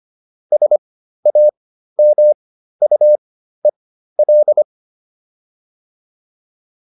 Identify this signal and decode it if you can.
Morse code, heard as a keyed tone, about 25 wpm, spaced as for 13 wpm, 600 Hz. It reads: SAMUEL
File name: QSO304 13